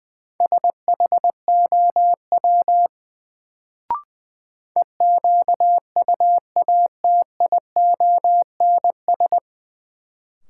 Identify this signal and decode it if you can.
Morse code, heard as a keyed tone, about 20 wpm, 700 Hz. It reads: SHOW EQUATIONS
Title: Sets of 5 Words - New Word Drill - 20wpm